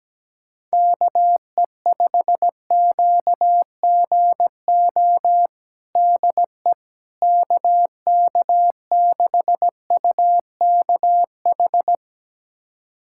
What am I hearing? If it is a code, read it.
Morse code, heard as a keyed tone, about 17 wpm, 700 Hz. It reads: KE5QGO DE KK6UKH
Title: USA Callsign DE Callsign - 6 Characters - 17wpm